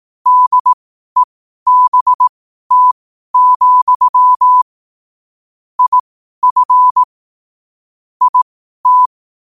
Morse code, heard as a keyed tone, about 18 wpm, spaced as for 12 wpm, 1000 Hz. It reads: DEBT, IF IT
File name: qotdfiles / 12wpm